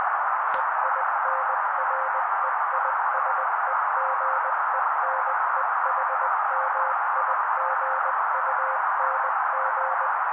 Morse code, heard as a keyed tone, about 20 wpm, 530 Hz. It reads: EINREISEGENEHMIGUNG
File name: daily / 18wpm noise